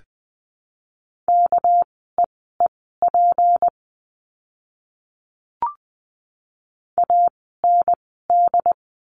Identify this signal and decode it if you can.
Morse code, heard as a keyed tone, about 20 wpm, spaced as for 14 wpm, 700 Hz. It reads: KEEP AND
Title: 237 Lesson 38f - Letter q - Words 20wpm